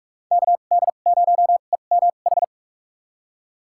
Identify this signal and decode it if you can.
Morse code, heard as a keyed tone, about 46 wpm, spaced as for 33 wpm, 700 Hz. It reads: KD0EMH
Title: USA Call Signs - 6 Characters - 2x Character Spacing - 45wpm